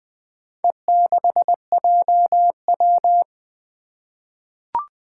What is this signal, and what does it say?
Morse code, heard as a keyed tone, about 20 wpm, 700 Hz. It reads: E6JW
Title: Four Challenging Letter-Number Combinations 20wpm